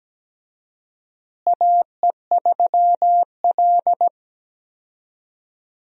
Morse code, heard as a keyed tone, about 17 wpm, 700 Hz. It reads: AE3L
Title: USA Call Signs - 4 Characters - Repeated 3x - 17wpm